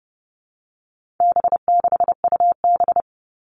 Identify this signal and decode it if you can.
Morse code, heard as a keyed tone, about 30 wpm, 700 Hz. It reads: B6UB
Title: All Cut Numbers - 4-Digit 30wpm